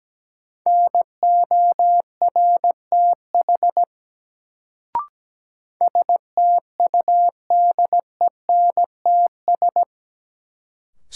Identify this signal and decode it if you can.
Morse code, heard as a keyed tone, about 17 wpm, 700 Hz. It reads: NORTH STUDENTS
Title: Sets of 5 Words Encore - New Word Drill - 17wpm